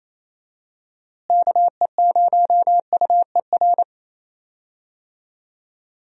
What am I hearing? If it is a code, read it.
Morse code, heard as a keyed tone, about 27 wpm, 700 Hz. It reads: KE0UER
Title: USA Call Signs - 6 Characters - Repeated 3x - 28wpm